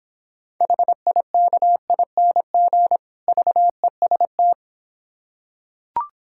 Morse code, heard as a keyed tone, about 26 wpm, 700 Hz. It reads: HIKING VEST